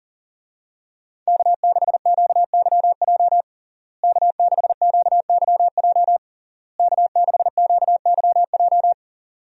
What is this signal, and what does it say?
Morse code, heard as a keyed tone, about 37 wpm, 700 Hz. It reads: K6QYJ K6QYJ K6QYJ
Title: USA Call Signs - 5 Characters - Repeated 3x - 40wpm